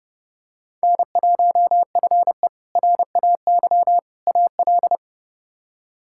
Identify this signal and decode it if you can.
Morse code, heard as a keyed tone, about 32 wpm, 700 Hz. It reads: N1FE RAY AL
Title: K1USN SST QSO Practice - 30wpm